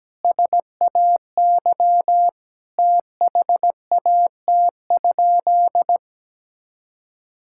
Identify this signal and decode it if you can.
Morse code, heard as a keyed tone, about 17 wpm, 700 Hz. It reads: SAY THAT?
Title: Sentences from Top 1000 Words - Review Part 5 - 17wpm